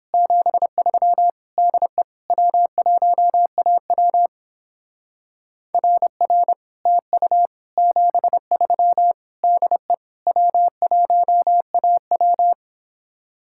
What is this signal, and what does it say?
Morse code, heard as a keyed tone, about 29 wpm, 700 Hz. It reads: 73 DE W1AW RR TU 73 DE W1AW